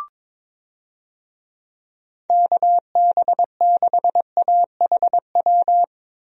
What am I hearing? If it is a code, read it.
Morse code, heard as a keyed tone, about 22 wpm, 700 Hz. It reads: KB6AHW
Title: USA Call Signs - 6 Characters - Repeated 3x - 22wpm